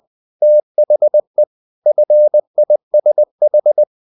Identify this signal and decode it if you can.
Morse code, heard as a keyed tone, about 20 wpm, 600 Hz. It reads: THE FISH
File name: the old man and the sea part 2 20wpm